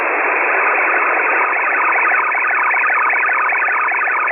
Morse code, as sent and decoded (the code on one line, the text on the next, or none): none